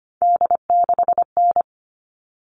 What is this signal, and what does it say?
Morse code, heard as a keyed tone, about 25 wpm, 700 Hz. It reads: D6N
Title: All Cut Numbers - 3-Digit 25wpm